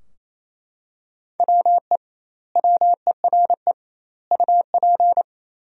Morse code, heard as a keyed tone, about 28 wpm, 700 Hz. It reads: WE WERE UP
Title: 2x Word-Spacing - Sentences from Top 400 Words - 28wpm